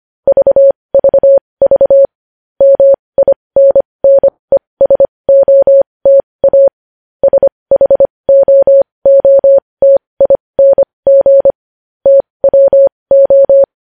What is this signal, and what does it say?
Morse code, heard as a keyed tone, about 25 wpm, spaced as for 20 wpm, 550 Hz. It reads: VVV MINNESOTA SHOOTING TWO
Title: News Headlines at 20